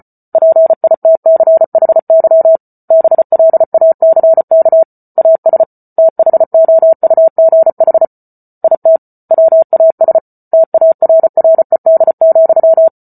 Morse code, heard as a keyed tone, about 34 wpm, 650 Hz. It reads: PITCHY BLACK AS THOUGH IT WAS TARRED,